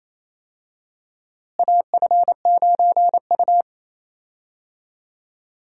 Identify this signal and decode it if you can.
Morse code, heard as a keyed tone, about 28 wpm, 700 Hz. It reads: AF9U
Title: USA Call Signs - 4 Characters - Repeated 2x - 28wpm